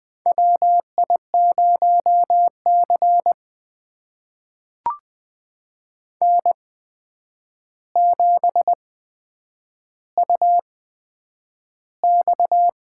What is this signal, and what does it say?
Morse code, heard as a keyed tone, about 20 wpm, spaced as for 9 wpm, 700 Hz. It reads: WI0C N7UX
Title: USA Call Signs - 4 Characters - 8x Character Spacing - 20wpm